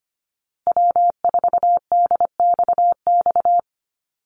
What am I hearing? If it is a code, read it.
Morse code, heard as a keyed tone, about 25 wpm, 700 Hz. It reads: W4DXX
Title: Call Signs Repeated 2x - 25wpm